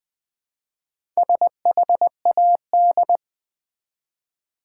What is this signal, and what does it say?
Morse code, heard as a keyed tone, about 20 wpm, 700 Hz. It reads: SHAD
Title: Build A Word - 1 Letter At A Time - 5-Letter Words - 20wpm